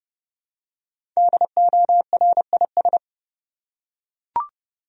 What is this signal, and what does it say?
Morse code, heard as a keyed tone, about 30 wpm, 700 Hz. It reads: DORIS
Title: USA Common Names - 30wpm